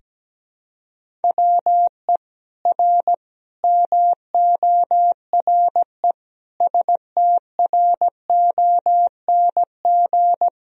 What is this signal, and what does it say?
Morse code, heard as a keyed tone, about 17 wpm, 700 Hz. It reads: WE R MORE STRONG
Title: Sentences from Top 500 Words and Abbreviations 17wpm